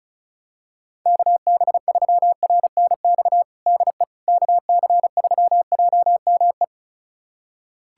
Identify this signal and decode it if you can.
Morse code, heard as a keyed tone, about 34 wpm, 700 Hz. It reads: KB3RNX DE KC3JME
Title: USA Callsign DE Callsign 35wpm